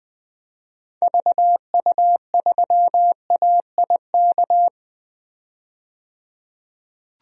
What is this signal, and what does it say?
Morse code, heard as a keyed tone, about 20 wpm, 700 Hz. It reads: VU3AIK
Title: DXCC-100 Call Signs - 6 Characters - 20wpm